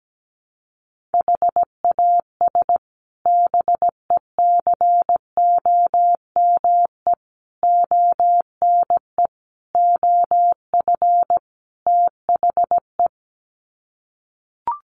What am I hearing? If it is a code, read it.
Morse code, heard as a keyed tone, about 17 wpm, 700 Hz. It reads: HAS BECOME ONE OF THE
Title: Sets of 5 Words Encore 17wpm